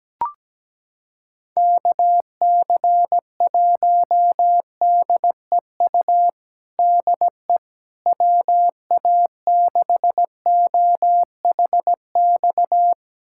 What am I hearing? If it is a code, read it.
Morse code, heard as a keyed tone, about 17 wpm, 700 Hz. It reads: KC1DEU DE WA6OHX
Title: USA Callsign DE Callsign - 6 Characters - 17wpm